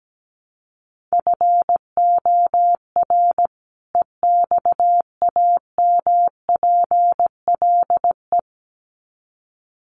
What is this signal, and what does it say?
Morse code, heard as a keyed tone, about 17 wpm, 700 Hz. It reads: FOR EXAMPLE
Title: Sets of 2 Words 17wpm